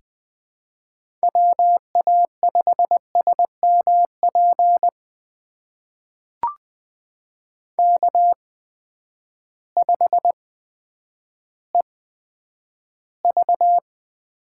Morse code, heard as a keyed tone, about 20 wpm, spaced as for 10 wpm, 700 Hz. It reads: WA5SMP K5EV
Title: USA Call Signs - 8x Character Spacing - 20wpm